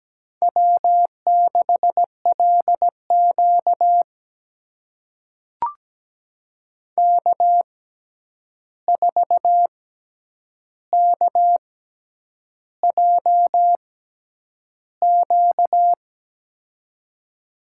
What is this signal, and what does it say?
Morse code, heard as a keyed tone, about 17 wpm, spaced as for 8 wpm, 700 Hz. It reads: W6LQ K4KJQ
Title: USA Call Signs - 6x Character Spacing - 17wpm